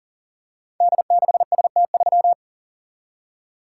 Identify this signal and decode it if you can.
Morse code, heard as a keyed tone, about 39 wpm, 700 Hz. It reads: D6ST3
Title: Code Groups - 5 Letter:Numbers - 40wpm